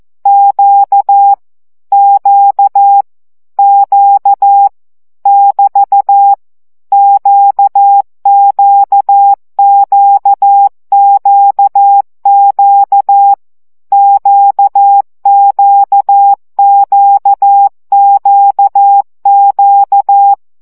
Morse code, heard as a keyed tone, about 14 wpm, 800 Hz. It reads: Q Q Q <BT> QQQQQ QQQQQ
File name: CW gyakorlas q 12 baud